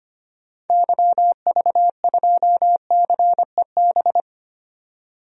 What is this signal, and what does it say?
Morse code, heard as a keyed tone, about 25 wpm, 690 Hz. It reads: YV2CEB
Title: DXCC-100 Call Signs - 6 Characters - 25wpm